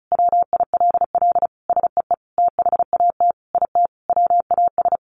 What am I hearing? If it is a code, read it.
Morse code, heard as a keyed tone, about 34 wpm, 700 Hz. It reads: WILL SEE THAT IT WAS
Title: Sentences from Top 800 Words 35wpm